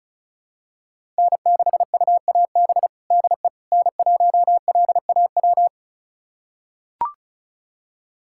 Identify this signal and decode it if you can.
Morse code, heard as a keyed tone, about 36 wpm, 700 Hz. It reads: N6UAB DE N1LAW
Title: USA Callsign DE Callsign - 5 Characters - 35wpm